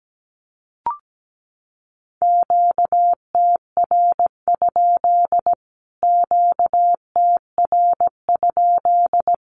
Morse code, heard as a keyed tone, about 17 wpm, 700 Hz. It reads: QTR? QTR?
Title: Q-Codes Repeated 3x 17wpm